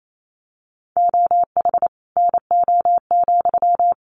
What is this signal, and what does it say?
Morse code, heard as a keyed tone, about 28 wpm, 700 Hz. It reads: OH NO,
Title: Sentences from Top 900 Words 28wpm